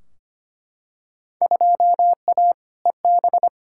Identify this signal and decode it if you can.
Morse code, heard as a keyed tone, about 25 wpm, 700 Hz. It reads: 2A EB
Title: Field Day Exchanges - 25wpm